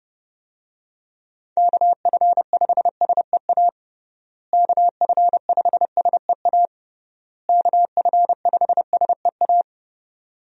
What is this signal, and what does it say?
Morse code, heard as a keyed tone, about 32 wpm, 700 Hz. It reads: KF5SEA KF5SEA KF5SEA
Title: USA Call Signs - 6 Characters - Repeated 3x - 30wpm